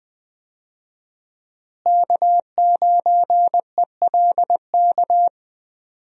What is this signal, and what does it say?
Morse code, heard as a keyed tone, about 20 wpm, 700 Hz. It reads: K9ELK
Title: USA Call Signs - 5 Characters - Repeated 2x - 20wpm